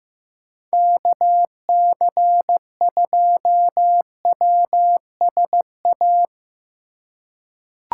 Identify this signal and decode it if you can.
Morse code, heard as a keyed tone, about 15 wpm, 700 Hz. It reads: KC2WSA
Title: USA Call Signs - 2x Character Spacing - 15wpm